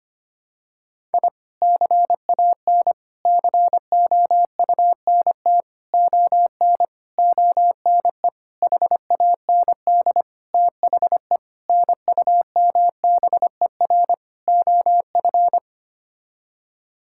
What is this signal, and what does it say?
Morse code, heard as a keyed tone, about 25 wpm, 700 Hz. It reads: I CAN COUNT ON ONE HAND THE NUMBER OF
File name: Sets of 9 Words - 25wpm